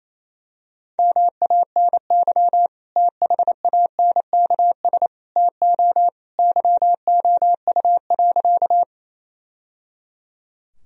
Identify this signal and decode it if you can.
Morse code, heard as a keyed tone, about 28 wpm, 700 Hz. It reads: MANY THANKS TO YOU.
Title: Sentences from Top 1000 Words - Review Part 3 - 28wpm